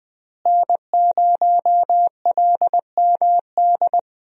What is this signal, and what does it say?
Morse code, heard as a keyed tone, about 21 wpm, 700 Hz. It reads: N0LMD
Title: USA Call Signs - 5 Characters - Repeated 2x - 20wpm